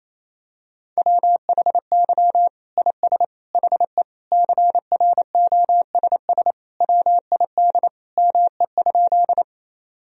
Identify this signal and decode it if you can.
Morse code, heard as a keyed tone, about 28 wpm, 700 Hz. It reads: WHY IS HE CROSS WID ME?
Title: Sentences from Top 300 Words and Abbreviations 28wpm